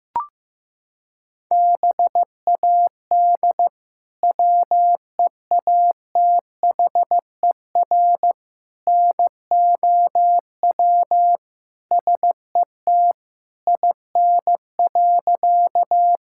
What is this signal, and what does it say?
Morse code, heard as a keyed tone, about 15 wpm, 700 Hz. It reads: BAD WEATHER NOW SET IN.
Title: Sentences from Top 1000 Words - Review Part 5 - 15wpm